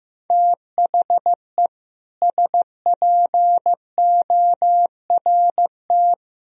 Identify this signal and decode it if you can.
Morse code, heard as a keyed tone, about 15 wpm, 700 Hz. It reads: THE SPORT
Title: Sentences from Top 1000 Words - Review Part 4 - 15wpm